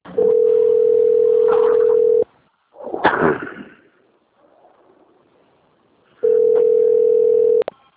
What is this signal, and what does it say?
Ringback tone (North America), 440 + 480 Hz.
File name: out- - - - - .47026